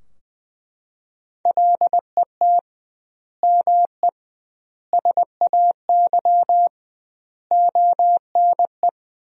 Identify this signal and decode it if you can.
Morse code, heard as a keyed tone, about 20 wpm, 700 Hz. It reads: LET ME SAY ONE
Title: 2x Word-Spacing - Sentences from Top 300 Words - 20wpm